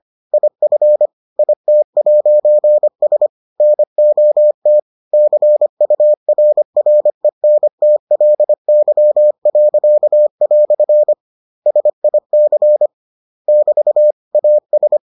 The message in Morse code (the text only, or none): IF IT'S NOT CURRENTLY." SIC <BT> AS